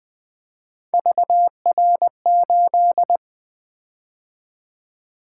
Morse code, heard as a keyed tone, about 20 wpm, 700 Hz. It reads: VR8
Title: Three Letter-Number Combinations - 20wpm